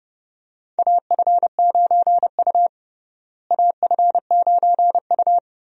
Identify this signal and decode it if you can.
Morse code, heard as a keyed tone, about 31 wpm, 700 Hz. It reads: AF9U AF9U
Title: USA Call Signs - 4 Characters - Repeated 2x - 30wpm